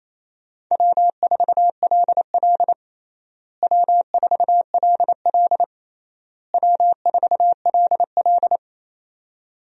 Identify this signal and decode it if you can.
Morse code, heard as a keyed tone, about 28 wpm, 700 Hz. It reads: W4LL W4LL W4LL
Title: USA Call Signs - 4 Characters - Repeated 3x - 28wpm